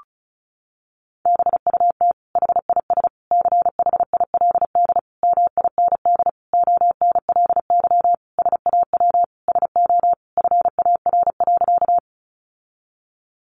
Morse code, heard as a keyed tone, about 34 wpm, 700 Hz. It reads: BUT HIS CHILD MIND ONLY SAW SO FAR.
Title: Sentences from Top 800 Words 35wpm